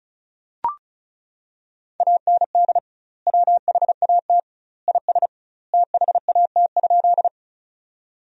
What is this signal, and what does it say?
Morse code, heard as a keyed tone, about 34 wpm, 700 Hz. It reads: AND WHAT IS THAT?
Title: 2x Word-Spacing - Sentences from Top 100 Words - 35wpm